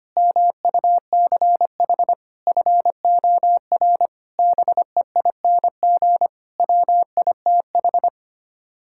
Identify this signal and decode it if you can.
Morse code, heard as a keyed tone, about 25 wpm, 700 Hz. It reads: MUCH FOR BEING WITH
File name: Sets of 9 Words - 25wpm